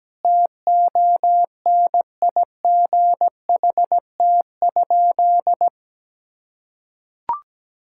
Morse code, heard as a keyed tone, about 17 wpm, 700 Hz. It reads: TONIGHT?
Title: Sentences from Top 1000 Words - Review Part 6 - 17wpm